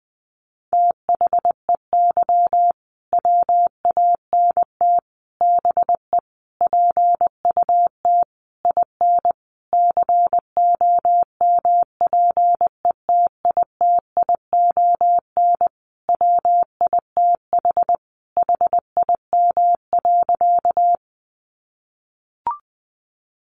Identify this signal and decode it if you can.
Morse code, heard as a keyed tone, about 20 wpm, 700 Hz. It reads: THEY WANT BE PUT IN COMPETITION WITH HIM.
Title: Sentences from Top 1000 Words 20wpm